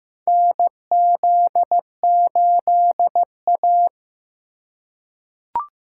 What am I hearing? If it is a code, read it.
Morse code, heard as a keyed tone, about 15 wpm, 700 Hz. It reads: NZ8A
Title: USA Call Signs - 4 Characters - 4x Character Spacing - 15wpm